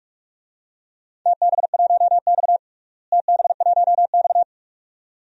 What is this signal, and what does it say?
Morse code, heard as a keyed tone, about 46 wpm, 700 Hz. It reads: TB1X TB1X
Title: DXCC-100 Call Signs - 4 Characters - Repeated 2x - 45wpm